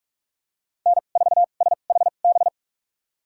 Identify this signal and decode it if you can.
Morse code, heard as a keyed tone, about 45 wpm, spaced as for 32 wpm, 700 Hz. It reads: N4SHB
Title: USA Call Signs - 2x Character Spacing - 45wpm